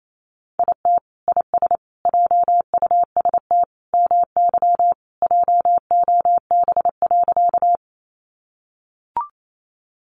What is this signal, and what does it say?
Morse code, heard as a keyed tone, about 28 wpm, 700 Hz. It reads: IT IS JUST MY JOB.
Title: Sentences from Top 600 Words 28wpm